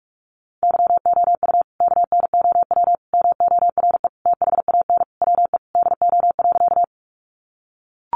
Morse code, heard as a keyed tone, about 44 wpm, 700 Hz. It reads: YOU KNOW MORE THAN WE DO.
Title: Sentences from Top 100 Words 45wpm